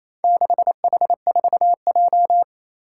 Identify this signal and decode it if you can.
Morse code, heard as a keyed tone, about 27 wpm, 700 Hz. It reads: 6H4J
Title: Four Challenging Letter-Number Combinations 28wpm